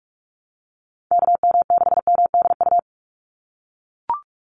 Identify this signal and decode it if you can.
Morse code, heard as a keyed tone, about 44 wpm, 700 Hz. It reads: KM6MDU
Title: USA Call Signs 45wpm